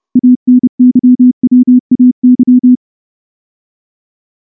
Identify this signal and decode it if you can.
Morse code, heard as a keyed tone, about 29 wpm, 250 Hz. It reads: ANYWAY
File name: morse / morse-training-30WPM